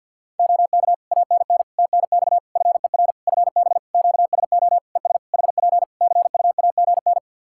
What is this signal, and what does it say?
Morse code, heard as a keyed tone, about 50 wpm, 700 Hz. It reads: OK ANN TNX FER FB QSO ES HP CUAGN
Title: Practice QSO - Exchange 4 - Ending - 50wpm